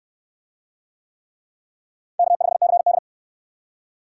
Morse code, heard as a keyed tone, about 61 wpm, 700 Hz. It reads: K5CD